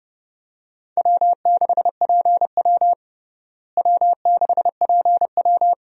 Morse code, heard as a keyed tone, about 30 wpm, 700 Hz. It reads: W6PW W6PW
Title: USA Call Signs - 4 Characters - Repeated 2x - 30wpm